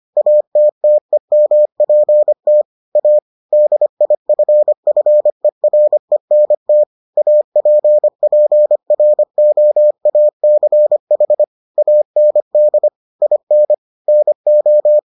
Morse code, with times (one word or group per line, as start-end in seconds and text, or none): 0.2-2.6 ATTEMPT
3.0-3.2 A
3.5-6.8 DIFFERENT
7.2-11.4 APPROACH
11.8-12.9 AND
13.2-13.7 IN
14.1-15.0 NO